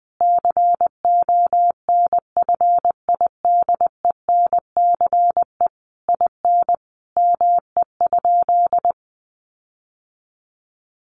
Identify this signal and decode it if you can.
Morse code, heard as a keyed tone, about 20 wpm, 700 Hz. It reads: CONFIDENCE IN ME?
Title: Sentences from Top 1500 Words 20wpm